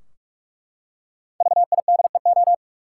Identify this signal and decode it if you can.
Morse code, heard as a keyed tone, about 42 wpm, 700 Hz. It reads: VIDEO
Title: Five Letter Words - 45wpm